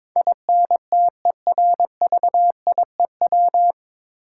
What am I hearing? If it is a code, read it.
Morse code, heard as a keyed tone, about 22 wpm, 700 Hz. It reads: INTERVIEW
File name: 9 Letter Words - First 3 Letters Given - 22wpm